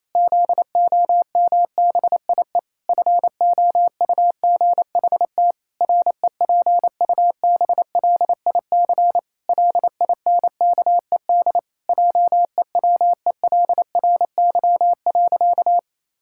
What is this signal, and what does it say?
Morse code, heard as a keyed tone, about 28 wpm, 700 Hz. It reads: ZOMBIE FOUGHT REPUBLIC LINKED JEWELRY.
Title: Pangram Challenge v2 - 28wpm